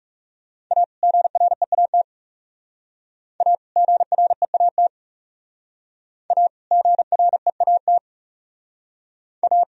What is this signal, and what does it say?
Morse code, heard as a keyed tone, about 38 wpm, 700 Hz. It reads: A GREAT A GREAT A GREAT A